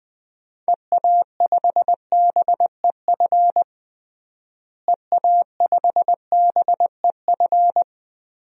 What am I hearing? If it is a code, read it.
Morse code, heard as a keyed tone, about 20 wpm, 700 Hz. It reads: EA5BEF EA5BEF
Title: DXCC-100 Call Signs - 6 Characters - Repeated 2x - 20wpm